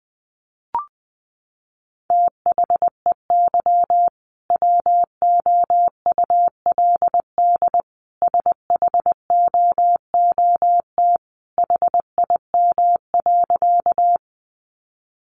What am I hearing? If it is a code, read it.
Morse code, heard as a keyed tone, about 20 wpm, 700 Hz. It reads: THEY WOULD SHOOT HIM.
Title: Sentences from Top 1200 Words 20wpm